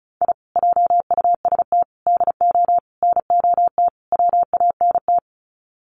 Morse code, heard as a keyed tone, about 34 wpm, 700 Hz. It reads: I JUST DO NOT WANT
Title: Sets of 5 Words Encore2 35wpm